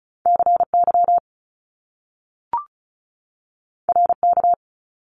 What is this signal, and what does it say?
Morse code, heard as a keyed tone, about 36 wpm, 700 Hz. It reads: CY RK
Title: Top Two Letter Combinations in Words 35wpm